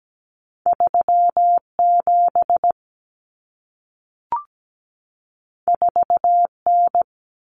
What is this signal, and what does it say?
Morse code, heard as a keyed tone, about 17 wpm, 700 Hz. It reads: 37 4N
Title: Common Cut Numbers - 2-Digit 17wpm